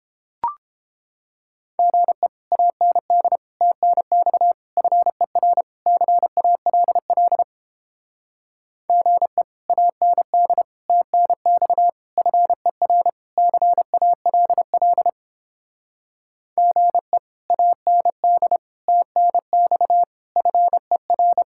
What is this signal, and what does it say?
Morse code, heard as a keyed tone, about 29 wpm, 700 Hz. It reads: GE AND TNX FER CALL GE AND TNX FER CALL GE AND TNX FER